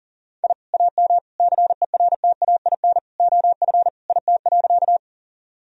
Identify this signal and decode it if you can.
Morse code, heard as a keyed tone, about 37 wpm, 700 Hz. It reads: I AM CERTAIN OF IT.
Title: Sentences from Top 1000 Words - Review Part 2 - 40wpm